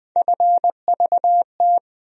Morse code, heard as a keyed tone, about 20 wpm, 700 Hz. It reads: FVT
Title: USA Call Sign Postfixes - 3 Letters 20wpm